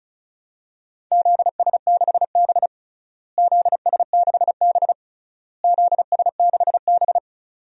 Morse code, heard as a keyed tone, about 36 wpm, 700 Hz. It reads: ZS6B ZS6B ZS6B
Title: DXCC-100 Call Signs - 4 Characters - Repeated 3x - 35wpm